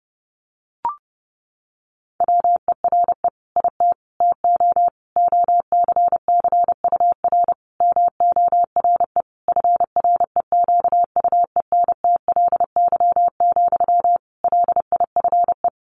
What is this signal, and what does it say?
Morse code, heard as a keyed tone, about 30 wpm, 700 Hz. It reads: WERE IT TO OCCUR MORE FREQUENTLY, LIFE
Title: Sentences from Top 1300 Words 30wpm